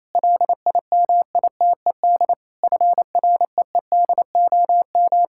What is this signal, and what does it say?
Morse code, heard as a keyed tone, about 28 wpm, 700 Hz. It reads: LIMITED FREEDOM
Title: Sentences from Top 1000 Words - Review Part 6 - 28wpm